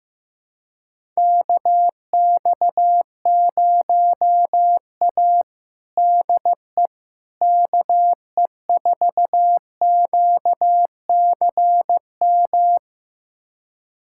Morse code, heard as a keyed tone, about 15 wpm, 700 Hz. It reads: KX0A DE KE4QCM
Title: USA Callsign DE Callsign 15wpm